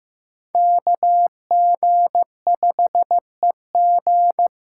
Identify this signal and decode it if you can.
Morse code, heard as a keyed tone, about 15 wpm, 700 Hz. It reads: KG5EG
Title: USA Call Signs - 6 Characters - 8x Character Spacing - 15wpm